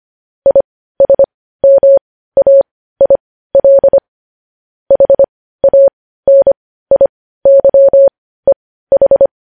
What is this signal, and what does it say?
Morse code, heard as a keyed tone, about 25 wpm, spaced as for 15 wpm, 550 Hz. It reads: ISMAIL HANIYEH
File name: podcast / News headlines-1x-25x15